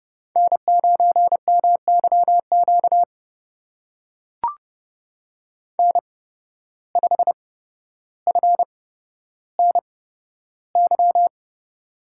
Morse code, heard as a keyed tone, about 30 wpm, spaced as for 13 wpm, 700 Hz. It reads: N9MYQ N5FNY
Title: USA Call Signs - 5 Characters - 8x Character Spacing - 30wpm